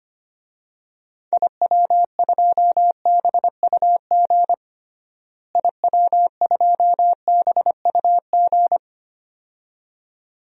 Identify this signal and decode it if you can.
Morse code, heard as a keyed tone, about 25 wpm, 700 Hz. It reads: IW2BUG IW2BUG
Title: DXCC-100 Call Signs - 6 Characters - Repeated 2x - 25wpm